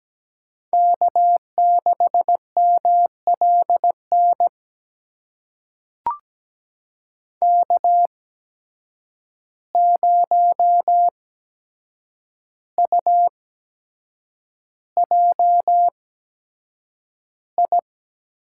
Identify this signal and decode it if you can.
Morse code, heard as a keyed tone, about 17 wpm, spaced as for 7 wpm, 700 Hz. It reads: K6MLN K0UJI
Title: USA Call Signs - 8x Character Spacing - 17wpm